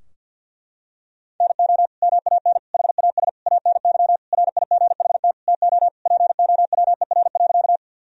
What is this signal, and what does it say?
Morse code, heard as a keyed tone, about 50 wpm, 700 Hz. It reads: NO MAN HAS ANY RIGHT TO POWER.
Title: Sentences from Top 1000 Words - Review Part 2 - 50wpm